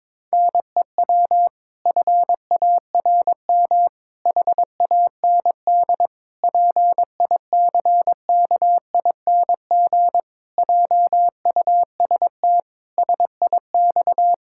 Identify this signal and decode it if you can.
Morse code, heard as a keyed tone, about 22 wpm, 700 Hz. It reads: NEW FARM HAND PICKING JUST SIX
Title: Pangram Challenge v2 - 22wpm